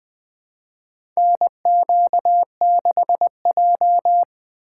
Morse code, heard as a keyed tone, about 20 wpm, 700 Hz. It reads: NQ6J
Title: USA Call Signs - 4 Characters - 3x Character Spacing - 20wpm